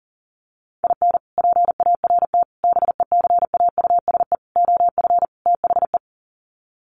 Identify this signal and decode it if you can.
Morse code, heard as a keyed tone, about 40 wpm, 700 Hz. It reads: IN PART BECAUSE OF THE
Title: Sets of 5 Words Encore2 40wpm